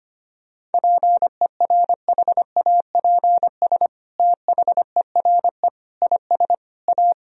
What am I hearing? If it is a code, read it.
Morse code, heard as a keyed tone, about 25 wpm, 700 Hz. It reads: PERHAPS THERE IS A
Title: Sentences from Top 1000 Words - Review Part 2 - 25wpm